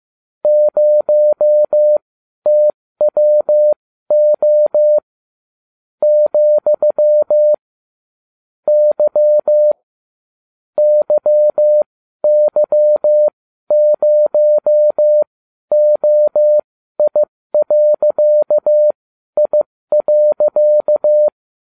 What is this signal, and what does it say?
Morse code, heard as a keyed tone, about 15 wpm, spaced as for 12 wpm, 600 Hz. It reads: 0TWO , Y YY0OI.I.